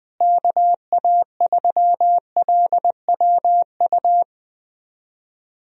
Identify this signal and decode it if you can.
Morse code, heard as a keyed tone, about 21 wpm, 700 Hz. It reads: KA3LWU
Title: USA Call Signs - 6 Characters - Repeated 2x - 20wpm